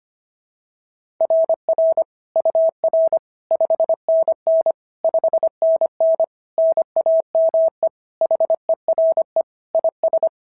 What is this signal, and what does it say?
Morse code, heard as a keyed tone, about 24 wpm, 650 Hz. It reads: RR UR 5NN 5NN NAME HERE IS
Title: SOTA QSO Practice 25wpm